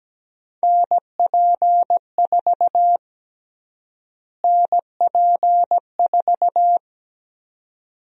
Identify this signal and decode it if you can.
Morse code, heard as a keyed tone, about 17 wpm, 700 Hz. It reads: NP4 NP4
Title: USA Call Sign Prefixes - Repeated 2x - 17wpm